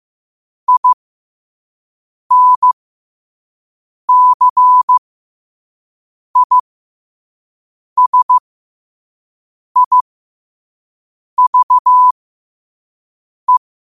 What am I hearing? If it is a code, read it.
Morse code, heard as a keyed tone, about 15 wpm, spaced as for 5 wpm, 1000 Hz. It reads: INCISIVE
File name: qotdfiles / 5wpm